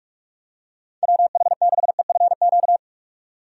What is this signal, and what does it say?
Morse code, heard as a keyed tone, about 45 wpm, 700 Hz. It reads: WH6EFQ
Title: USA Call Signs - 6 Characters - 45wpm